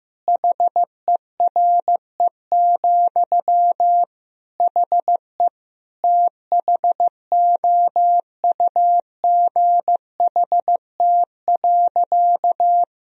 Morse code, heard as a keyed tone, about 15 wpm, 700 Hz. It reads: HERE, HE THOUGHT.
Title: Sentences from Top 1000 Words - Review Part 3 - 15wpm